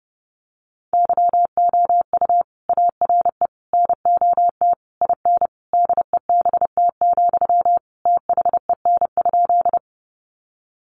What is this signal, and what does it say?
Morse code, heard as a keyed tone, about 30 wpm, 700 Hz. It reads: YOU ARE NOT IN DEBT, THEN?
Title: Sentences from Top 1300 Words 30wpm